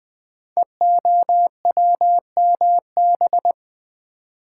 Morse code, heard as a keyed tone, about 20 wpm, 700 Hz. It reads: EOWMB
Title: Code Groups - 5 Letter:Numbers - 20wpm